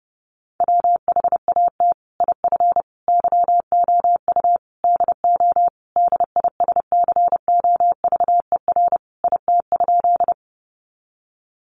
Morse code, heard as a keyed tone, about 28 wpm, 700 Hz. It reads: WHAT IF YOU DO DISCOVER IT?
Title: Sentences from Top 1100 Words 30wpm